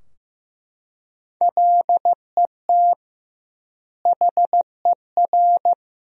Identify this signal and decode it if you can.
Morse code, heard as a keyed tone, about 15 wpm, 700 Hz. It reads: LET HER
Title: 2x Word-Spacing - Sentences from Top 300 Words - 15wpm